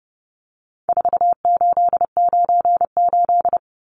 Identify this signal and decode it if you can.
Morse code, heard as a keyed tone, about 30 wpm, 700 Hz. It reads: 4898
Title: Numbers - 4-Digit 30wpm